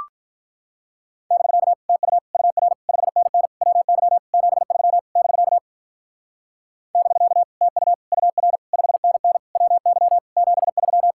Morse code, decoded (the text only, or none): <BK> TU UR 5NN WY 73 <BK> <BK> TU UR 5NN WY 73